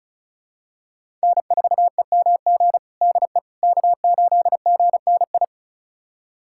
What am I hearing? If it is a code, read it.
Morse code, heard as a keyed tone, about 36 wpm, 700 Hz. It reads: N4EMG DE K8GNI
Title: USA Callsign DE Callsign - 5 Characters - 35wpm